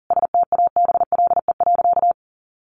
Morse code, heard as a keyed tone, about 40 wpm, 700 Hz. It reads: STABLE.
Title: Sentences from Top 1500 Words 40wpm